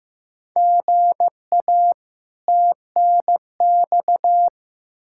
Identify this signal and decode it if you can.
Morse code, heard as a keyed tone, about 15 wpm, 700 Hz. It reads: GA TNX
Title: Practice QSO - Exchange 1 - Beginning - 15wpm